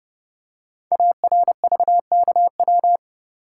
Morse code, heard as a keyed tone, about 29 wpm, 690 Hz. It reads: ARVKW
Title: Code Groups - 5 Letters - 30wpm